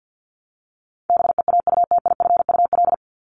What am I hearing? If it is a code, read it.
Morse code, heard as a keyed tone, about 50 wpm, 700 Hz. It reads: BEAUTIFUL
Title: Top 751-1000 Words 50wpm